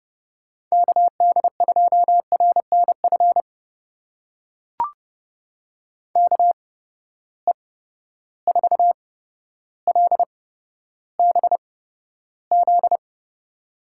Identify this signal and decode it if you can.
Morse code, heard as a keyed tone, about 30 wpm, spaced as for 13 wpm, 700 Hz. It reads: KD2RNF KE4LBZ
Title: USA Call Signs - 6 Characters - 8x Character Spacing - 30wpm